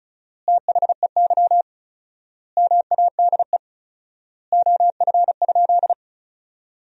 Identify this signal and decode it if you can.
Morse code, heard as a keyed tone, about 36 wpm, 700 Hz. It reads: THEY MADE OF?
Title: 4x Word-Spacing - Sentences from Top 200 Words - 35wpm